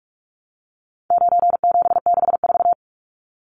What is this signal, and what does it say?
Morse code, heard as a keyed tone, about 44 wpm, 700 Hz. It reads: 9764
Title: Numbers - 4-Digit 45wpm